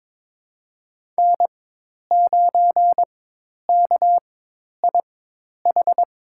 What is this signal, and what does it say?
Morse code, heard as a keyed tone, about 21 wpm, spaced as for 10 wpm, 700 Hz. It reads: N9KIH
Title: USA Call Signs - 5 Characters - 4x Character Spacing - 22wpm